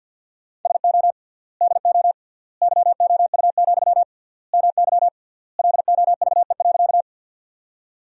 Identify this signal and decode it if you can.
Morse code, heard as a keyed tone, about 50 wpm, 700 Hz. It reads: SO DO YOU, MY LOVE.
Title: 3x Word-Spacing - Sentences from Top 400 Words - 50wpm